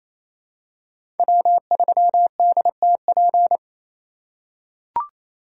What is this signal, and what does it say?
Morse code, heard as a keyed tone, about 28 wpm, 700 Hz. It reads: W3DTP
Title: USA Call Signs - 5 Characters - 2x Character Spacing - 28wpm